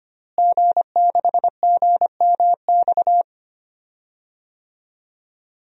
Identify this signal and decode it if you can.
Morse code, heard as a keyed tone, about 25 wpm, 700 Hz. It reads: G6GMX
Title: DXCC-100 Call Signs - 5 Characters - Repeated 3x - 25wpm